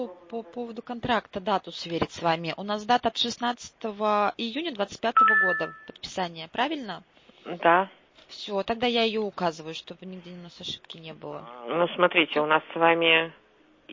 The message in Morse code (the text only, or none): none